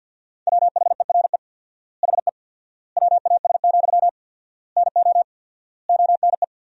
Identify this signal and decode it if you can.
Morse code, heard as a keyed tone, about 50 wpm, 700 Hz. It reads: WHERE HE WAS, NO ONE
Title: 4x Word-Spacing - Sentences from Top 400 Words - 50wpm